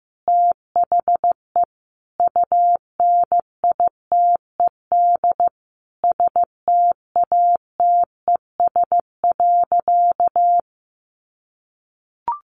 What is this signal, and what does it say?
Morse code, heard as a keyed tone, about 15 wpm, 700 Hz. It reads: THE UNITED STATES.
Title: A Sound Of Thunder 15wpm